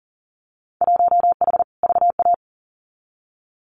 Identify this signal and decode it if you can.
Morse code, heard as a keyed tone, about 40 wpm, 700 Hz. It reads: 1H VA
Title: Winter Field Day Exchanges 40wpm